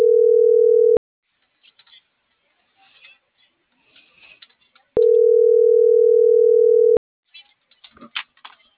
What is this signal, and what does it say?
Ringback tone (North America), 440 + 480 Hz.
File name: CENTRAL NMP / out patricia.martineznmp - .42095